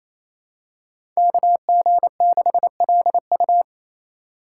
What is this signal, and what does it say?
Morse code, heard as a keyed tone, about 28 wpm, 700 Hz. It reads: KG6LU
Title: USA Call Signs - 5 Characters - 8x Character Spacing - 28wpm